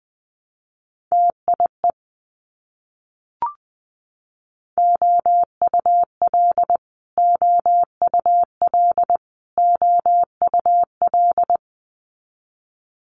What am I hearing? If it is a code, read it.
Morse code, heard as a keyed tone, about 21 wpm, 700 Hz. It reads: TIE OUL OUL OUL
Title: Multi-letter Phonemes Repeated 3x 20wpm